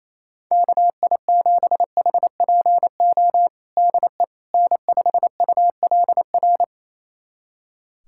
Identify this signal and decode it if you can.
Morse code, heard as a keyed tone, about 28 wpm, 700 Hz. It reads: KI7HPO DE N5ULR
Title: USA Callsign DE Callsign 28wpm